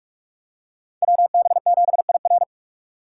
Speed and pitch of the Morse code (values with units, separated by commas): 43 wpm, 700 Hz